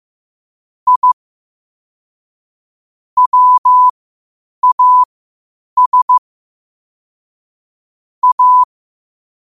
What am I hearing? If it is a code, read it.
Morse code, heard as a keyed tone, about 15 wpm, spaced as for 8 wpm, 1000 Hz. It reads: I WAS A
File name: qotdfiles / 8wpm